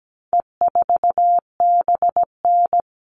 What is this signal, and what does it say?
Morse code, heard as a keyed tone, about 17 wpm, 700 Hz. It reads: E4BN
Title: All Cut Numbers - 4-Digit 17wpm